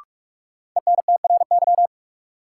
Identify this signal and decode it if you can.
Morse code, heard as a keyed tone, about 44 wpm, 700 Hz. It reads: ENTRY
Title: Sentences from Top 1500 Words - New Word Drill Alt - 45wpm